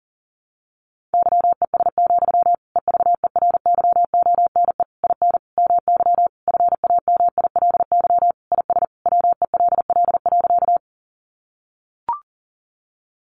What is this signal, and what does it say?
Morse code, heard as a keyed tone, about 37 wpm, 700 Hz. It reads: YES, EVERYONE IN MY FAMILY IS WELL.
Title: Sentences from Top 800 Words 40wpm